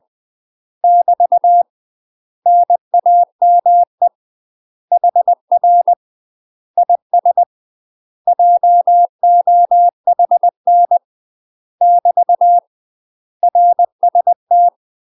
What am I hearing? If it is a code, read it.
Morse code, heard as a keyed tone, about 20 wpm, 700 Hz. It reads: <BT> NAME HR IS JOHN <BT> RST
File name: UK MASTER QSOs TRAINING / 20WPM / UK MASTER QSOs 0053